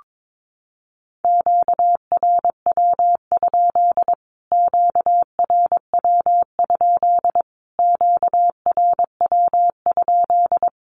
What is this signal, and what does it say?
Morse code, heard as a keyed tone, about 22 wpm, 700 Hz. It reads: QRW? QRW? QRW?
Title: Q-Codes Repeated 3x 22wpm